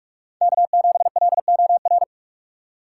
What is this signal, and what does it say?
Morse code, heard as a keyed tone, about 46 wpm, 700 Hz. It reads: K7LOR
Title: USA Call Signs - 5 Characters - 6x Character Spacing - 45wpm